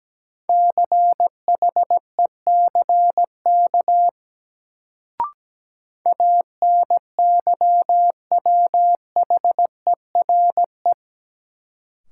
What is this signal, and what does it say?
Morse code, heard as a keyed tone, about 17 wpm, 700 Hz. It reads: CHECK ANYWHERE
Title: Sentences from Top 1400 Words - New Word Drill Alt - 17wpm